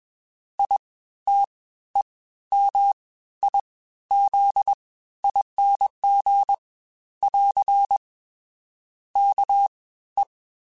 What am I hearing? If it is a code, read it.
Morse code, heard as a keyed tone, about 21 wpm, spaced as for 13 wpm, 800 Hz. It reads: ITEMIZING <AR> KE